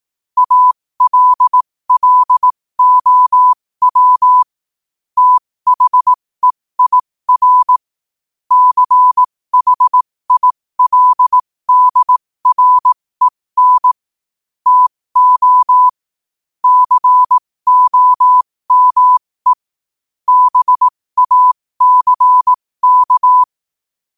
Morse code, heard as a keyed tone, about 18 wpm, spaced as for 15 wpm, 1000 Hz. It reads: ALLOW THEIR CHILDREN TO COME BACK
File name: qotdfiles / 15wpm